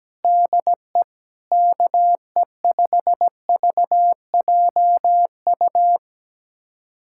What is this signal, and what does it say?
Morse code, heard as a keyed tone, about 17 wpm, 700 Hz. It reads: DE KE5VJU
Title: USA Callsign DE Callsign 17wpm